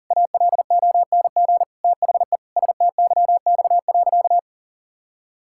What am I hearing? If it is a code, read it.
Morse code, heard as a keyed tone, about 37 wpm, 700 Hz. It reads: ALONG THE STYX.
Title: Pangram Challenge v2 - 40wpm